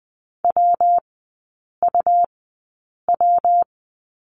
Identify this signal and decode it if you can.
Morse code, heard as a keyed tone, about 20 wpm, spaced as for 8 wpm, 700 Hz. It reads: WUW
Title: 114 Lesson 19e - Letter w - Three Characters 20wpm